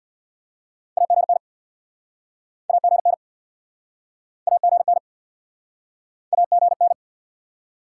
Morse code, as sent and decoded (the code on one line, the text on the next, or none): .- --. -. / .- --. -. / .- --. -. / .- --. -.
AGN AGN AGN AGN